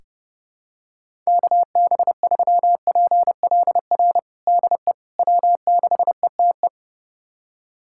Morse code, heard as a keyed tone, about 32 wpm, 700 Hz. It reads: KB3PLR DE W6ETE
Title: USA Callsign DE Callsign 30wpm